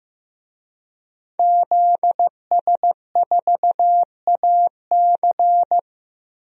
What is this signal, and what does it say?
Morse code, heard as a keyed tone, about 15 wpm, 700 Hz. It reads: ZS4AC
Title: DXCC-100 Call Signs - 5 Characters - Repeated 2x - 15wpm